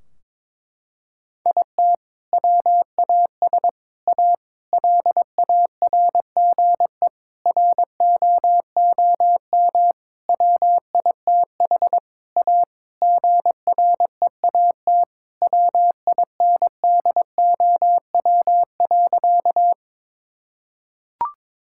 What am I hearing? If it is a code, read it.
Morse code, heard as a keyed tone, about 22 wpm, 700 Hz. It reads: IT WAS A LARGE ROOM WITH A GREAT WINDOW.
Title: Sentences from Top 1600 Words 22wpm